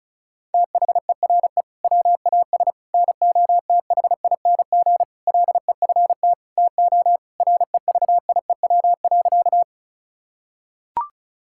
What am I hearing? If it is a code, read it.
Morse code, heard as a keyed tone, about 36 wpm, 700 Hz. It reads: THERE WAS NOTHING LEFT TO REVIEW.
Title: Sentences from Top 1000 Words - Review Part 6 - 35wpm